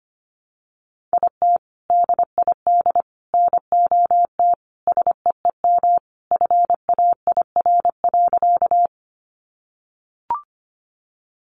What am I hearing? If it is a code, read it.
Morse code, heard as a keyed tone, about 25 wpm, 700 Hz. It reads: IT DID NOT SEEM FAIR.
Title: Sentences from Top 1300 Words 25wpm